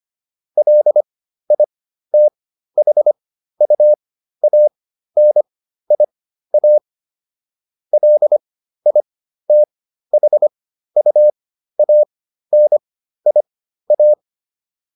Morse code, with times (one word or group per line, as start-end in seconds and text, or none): 0.6-6.8 LITHUANIA
7.9-14.1 LITHUANIA